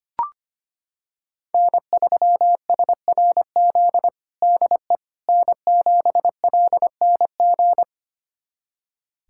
Morse code, heard as a keyed tone, about 25 wpm, 700 Hz. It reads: N3SRZ DE N7LNG
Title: USA Callsign DE Callsign - 5 Characters - 25wpm